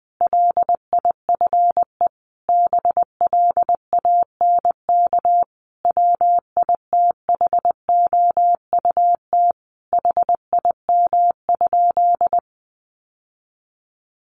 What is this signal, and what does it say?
Morse code, heard as a keyed tone, about 19 wpm, 700 Hz. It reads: LIFE BLANK WITHOUT HIM?
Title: Sentences from Top 1500 Words 20wpm